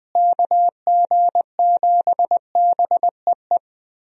Morse code, heard as a keyed tone, about 20 wpm, 700 Hz. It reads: KG7BEE
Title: USA Call Signs - 6 Characters - Repeated 3x - 20wpm